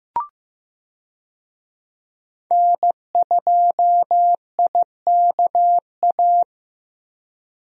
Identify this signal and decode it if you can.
Morse code, heard as a keyed tone, about 15 wpm, 700 Hz. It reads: N2IKA
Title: USA Call Signs - 5 Characters - 15wpm